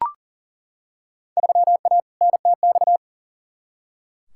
Morse code, heard as a keyed tone, about 40 wpm, 700 Hz. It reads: 3A NTX
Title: Field Day Exchanges - 40wpm